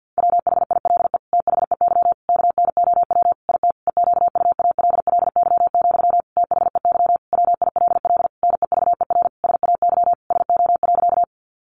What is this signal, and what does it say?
Morse code, heard as a keyed tone, about 50 wpm, 700 Hz. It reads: WHILE THEY KNOW IT EQUALLY, THEY WILL NEVER SAY SO.
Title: Sentences from Top 1400 Words 50wpm